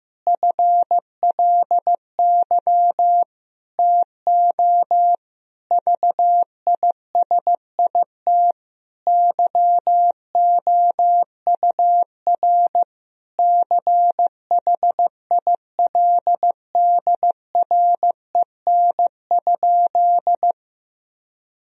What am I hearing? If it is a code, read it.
Morse code, heard as a keyed tone, about 15 wpm, 700 Hz. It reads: FLY TO VISIT YOUR CHILDREN?
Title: Sentences from Top 1000 Words - Review Part 2 - 15wpm